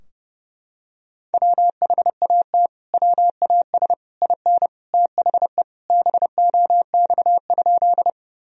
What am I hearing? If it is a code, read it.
Morse code, heard as a keyed tone, about 30 wpm, 700 Hz. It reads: WHAT WAS IN THE BOX?
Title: Sentences from Top 1000 Words - Review Part 2 - 30wpm